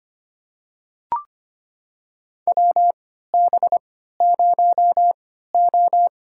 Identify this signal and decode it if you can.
Morse code, heard as a keyed tone, about 25 wpm, spaced as for 14 wpm, 700 Hz. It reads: WB0O
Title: USA Call Signs - 4 Characters - 3x Character Spacing - 25wpm